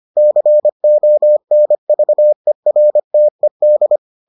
Morse code, heard as a keyed tone, about 25 wpm, 600 Hz. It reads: CONVERTED